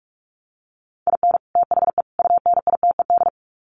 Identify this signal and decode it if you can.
Morse code, heard as a keyed tone, about 44 wpm, 700 Hz. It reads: IN THE UNITED
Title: Sets of 3 Words 45wpm